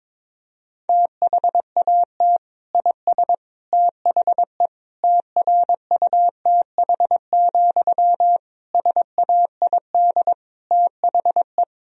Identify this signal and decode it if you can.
Morse code, heard as a keyed tone, about 22 wpm, 700 Hz. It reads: THAT IS THE TRUTH, SAID THE
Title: Sentences from Top 1000 Words - Review Part 1 - 22wpm